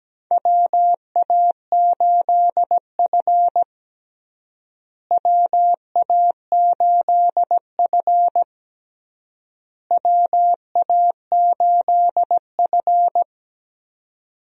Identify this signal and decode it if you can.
Morse code, heard as a keyed tone, about 17 wpm, 700 Hz. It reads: WA8F WA8F WA8F
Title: USA Call Signs - 4 Characters - Repeated 3x - 17wpm